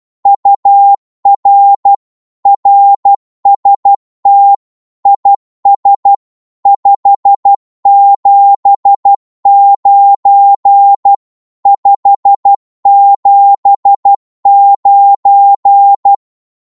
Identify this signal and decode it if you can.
Morse code, heard as a keyed tone, about 12 wpm, 800 Hz. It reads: UR RST IS 579 579
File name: Morse / qso-12wpm